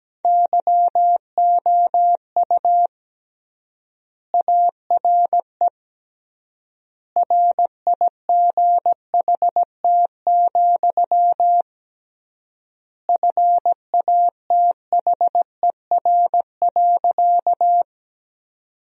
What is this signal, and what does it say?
Morse code, heard as a keyed tone, about 17 wpm, 700 Hz. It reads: YOU ARE RIGHT, FATHER.
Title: 3x Word-Spacing - Sentences from Top 300 Words - 17wpm